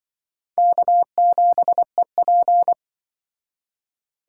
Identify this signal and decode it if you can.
Morse code, heard as a keyed tone, about 24 wpm, 700 Hz. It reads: K7EP
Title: Speed Racing - USA Call Signs 20wpm